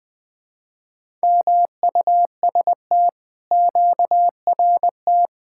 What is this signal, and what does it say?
Morse code, heard as a keyed tone, about 19 wpm, 700 Hz. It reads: MUST QRT
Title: Speed-Racing Common QSO Phrases - 20wpm